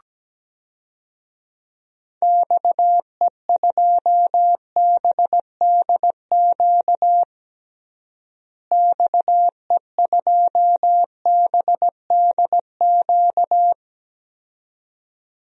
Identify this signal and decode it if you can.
Morse code, heard as a keyed tone, about 17 wpm, 700 Hz. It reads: XE2BDQ XE2BDQ
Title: DXCC-100 Call Signs - 6 Characters - Repeated 2x - 17wpm